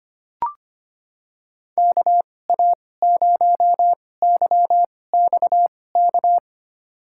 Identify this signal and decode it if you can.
Morse code, heard as a keyed tone, about 25 wpm, spaced as for 18 wpm, 700 Hz. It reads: KA0YXK
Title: USA Call Signs - 6 Characters - 2x Character Spacing - 25wpm